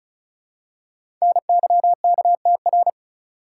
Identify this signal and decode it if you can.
Morse code, heard as a keyed tone, about 36 wpm, 700 Hz. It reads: NYKTR
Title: Code Groups - 5 Letters - 35wpm